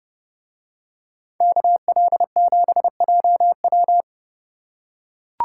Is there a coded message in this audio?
Yes: Morse code from 1.4 to 4.0 s, heard as a keyed tone, about 29 wpm, 700 Hz. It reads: KL7JW